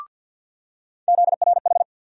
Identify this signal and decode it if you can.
Morse code, heard as a keyed tone, about 53 wpm, 700 Hz. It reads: ZRH
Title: USA Call Sign Postfixes - 3 Letters 50wpm